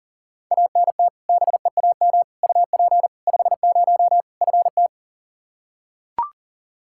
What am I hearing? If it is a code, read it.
Morse code, heard as a keyed tone, about 40 wpm, 700 Hz. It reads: ANT BEAM UP 50 FT
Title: K1USN SST QSO Practice 40wpm